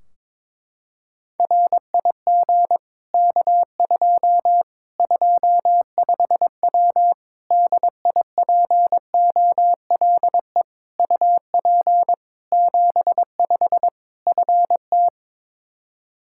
Morse code, heard as a keyed tone, about 22 wpm, 700 Hz. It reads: RIG K2 25W DIPOLE UP 75 FT
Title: Practice QSO - Exchange 2 - Info - 22wpm